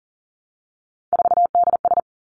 Morse code, heard as a keyed tone, about 42 wpm, 690 Hz. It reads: 4DS
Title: Three Challenging Letter-Number Combinations 40wpm